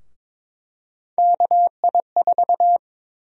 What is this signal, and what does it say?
Morse code, heard as a keyed tone, about 22 wpm, 700 Hz. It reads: KI4
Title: USA Call Sign Prefixes - Repeated 2x - 22wpm